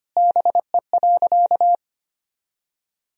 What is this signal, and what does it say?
Morse code, heard as a keyed tone, about 25 wpm, 700 Hz. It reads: BE.
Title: Sentences from Top 200 Words and Abbreviations 25wpm